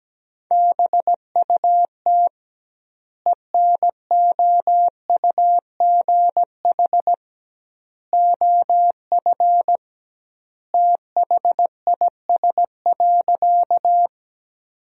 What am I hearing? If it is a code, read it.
Morse code, heard as a keyed tone, about 17 wpm, 700 Hz. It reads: BUT ENOUGH OF THIS.
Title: 2x Word-Spacing - Sentences from Top 400 Words - 17wpm